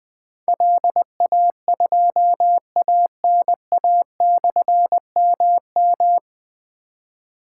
Morse code, heard as a keyed tone, about 20 wpm, 700 Hz. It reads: LA2ANA/MM
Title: Difficult Call Signs 20wpm